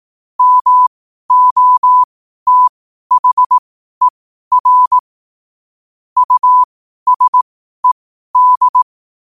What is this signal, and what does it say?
Morse code, heard as a keyed tone, about 18 wpm, spaced as for 12 wpm, 1000 Hz. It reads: MOTHER USED
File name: qotdfiles / 12wpm